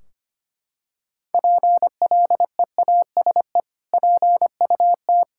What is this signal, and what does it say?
Morse code, heard as a keyed tone, about 25 wpm, 700 Hz. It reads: PLEASE PUT
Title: Sentences from Top 1600 Words 25wpm